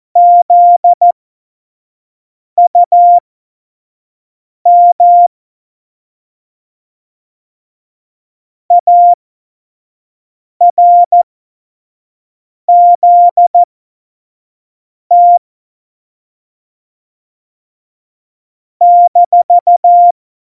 Morse code, heard as a keyed tone, about 14 wpm, spaced as for 5 wpm, 700 Hz. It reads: ZUM ARZT -
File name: Morsekurs / Witze / Witz 26. Dezember 2017, 25 F70